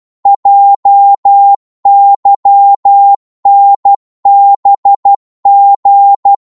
Morse code, heard as a keyed tone, about 12 wpm, 800 Hz. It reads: JYNBG
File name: Morse / ltr-12wpm